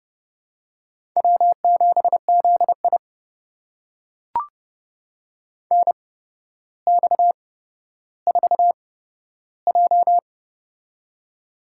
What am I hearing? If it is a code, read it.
Morse code, heard as a keyed tone, about 32 wpm, spaced as for 13 wpm, 700 Hz. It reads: W7ZI NX4J
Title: USA Call Signs - 4 Characters - 8x Character Spacing - 30wpm